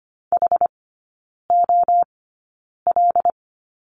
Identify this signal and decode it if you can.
Morse code, heard as a keyed tone, about 25 wpm, spaced as for 9 wpm, 700 Hz. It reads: HOL
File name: course / 25:10wpm / 093 Lesson 16e - Letter c - Three Characters 25wpm